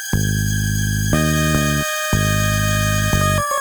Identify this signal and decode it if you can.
There is no code.